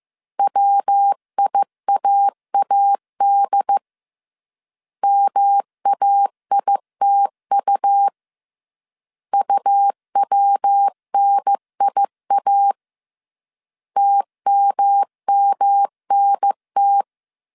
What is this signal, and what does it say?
Morse code, heard as a keyed tone, about 15 wpm, 800 Hz. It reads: WIAAD MAITU UWNIA TMMNT